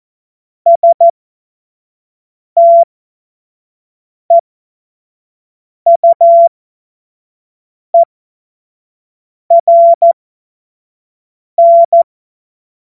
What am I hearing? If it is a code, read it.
Morse code, heard as a keyed tone, about 14 wpm, spaced as for 5 wpm, 680 Hz. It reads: STEUERN
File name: Morsekurs / Witze / Witz 11. April 2023 25 F70